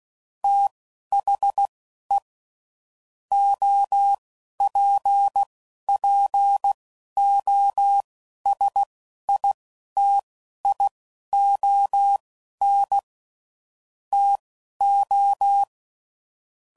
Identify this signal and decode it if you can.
Morse code, heard as a keyed tone, about 16 wpm, spaced as for 11 wpm, 800 Hz. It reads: THE OPPOSITION TO
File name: CW Code Groups @10WPM / Technical Subjects / XL